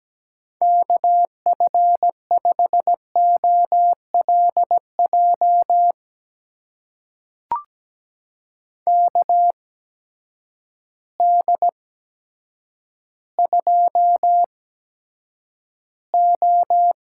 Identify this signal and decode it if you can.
Morse code, heard as a keyed tone, about 17 wpm, spaced as for 8 wpm, 700 Hz. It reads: KF5OLJ KD2O
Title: USA Call Signs - 6 Characters - 8x Character Spacing - 17wpm